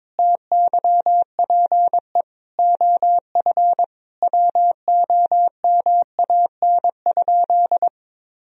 Morse code, heard as a keyed tone, about 22 wpm, 700 Hz. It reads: TYPE OF WOMAN?
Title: Sentences from Top 1000 Words - Review Part 3 - 22wpm